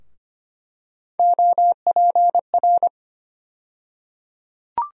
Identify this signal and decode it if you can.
Morse code, heard as a keyed tone, about 25 wpm, 700 Hz. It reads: OPR
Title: USA Call Sign Postfixes - 3 Letters 25wpm